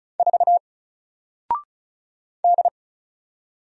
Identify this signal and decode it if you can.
Morse code, heard as a keyed tone, about 36 wpm, 700 Hz. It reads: 4 D
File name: Single-Letter-Number-35wpm